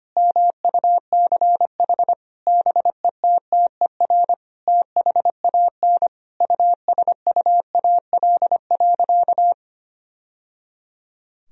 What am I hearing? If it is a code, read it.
Morse code, heard as a keyed tone, about 25 wpm, 700 Hz. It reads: MUCH BETTER THAN USUAL.
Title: Sentences from Top 1000 Words - Review Part 2 - 25wpm